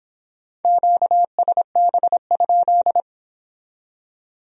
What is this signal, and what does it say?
Morse code, heard as a keyed tone, about 26 wpm, 700 Hz. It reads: QSB?